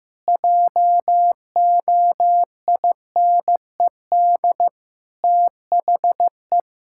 Morse code, heard as a keyed tone, about 15 wpm, 700 Hz. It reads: JOINED THE
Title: Pangram Challenge v2 - 15wpm